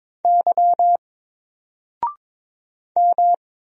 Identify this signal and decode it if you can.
Morse code, heard as a keyed tone, about 22 wpm, 700 Hz. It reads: Y M
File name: Single-Letter-Number-22wpm